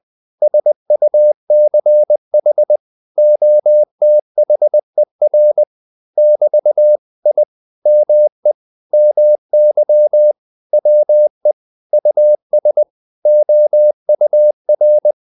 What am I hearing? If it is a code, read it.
Morse code, heard as a keyed tone, about 20 wpm, 600 Hz. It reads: SUCH OTHER <BT> I ME MY WE US OUR
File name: CW / Practice / 500MostCommonEnglishWords20